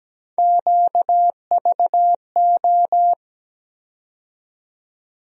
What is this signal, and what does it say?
Morse code, heard as a keyed tone, about 17 wpm, 700 Hz. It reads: QVO
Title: USA Call Sign Postfixes - 3 Letters 17wpm